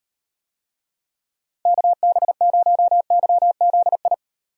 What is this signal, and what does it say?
Morse code, heard as a keyed tone, about 38 wpm, 700 Hz. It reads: KB0YZI